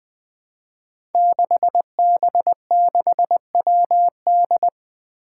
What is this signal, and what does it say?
Morse code, heard as a keyed tone, about 20 wpm, 700 Hz. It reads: 6B6WD
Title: Five Challenging Letter-Number Combinations 20wpm